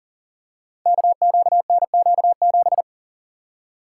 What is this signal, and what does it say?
Morse code, heard as a keyed tone, about 37 wpm, 700 Hz. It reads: KQNQ7
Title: Code Groups - 5 Letter:Numbers - 40wpm